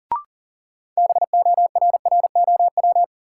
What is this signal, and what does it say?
Morse code, heard as a keyed tone, about 40 wpm, 700 Hz. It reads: BORROW
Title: Sentences from Top 1500 Words - New Word Drill Alt - 40wpm